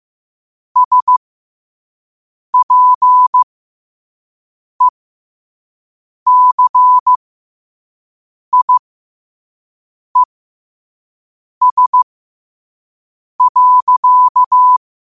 Morse code, heard as a keyed tone, about 15 wpm, spaced as for 5 wpm, 1000 Hz. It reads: SPECIES.
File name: qotdfiles / 5wpm